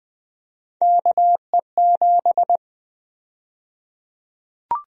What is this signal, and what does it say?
Morse code, heard as a keyed tone, about 20 wpm, 700 Hz. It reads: KE7
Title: USA Call Sign Prefixes - 20wpm